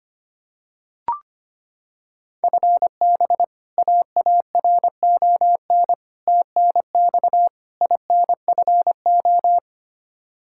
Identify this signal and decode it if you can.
Morse code, heard as a keyed tone, about 25 wpm, 700 Hz. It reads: FB AARON TNX INFO
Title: K1USN SST QSO Practice 25wpm